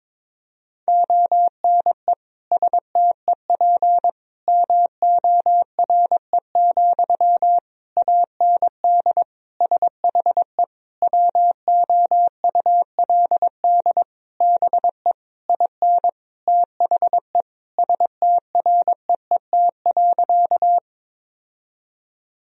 Morse code, heard as a keyed tone, about 22 wpm, 700 Hz. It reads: ONE STEP MORE, AND SHE WOULD BE IN THE STREET.
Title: Sentences from Top 1000 Words - Review Part 3 - 22wpm